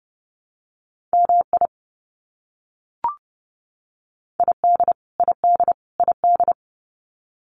Morse code, heard as a keyed tone, about 30 wpm, 700 Hz. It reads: MI ID ID ID
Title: US States Repeated 3x 30wpm